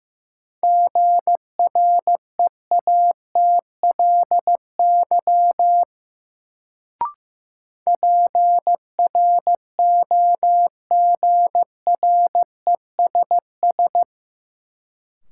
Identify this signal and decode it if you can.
Morse code, heard as a keyed tone, about 15 wpm, 700 Hz. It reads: GREATLY PROGRESS
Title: Sentences from Top 1400 Words - New Word Drill - 15wpm